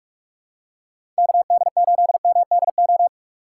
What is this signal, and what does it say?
Morse code, heard as a keyed tone, about 46 wpm, 700 Hz. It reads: KD8MDO
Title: USA Call Signs - 6 Characters - 2x Character Spacing - 45wpm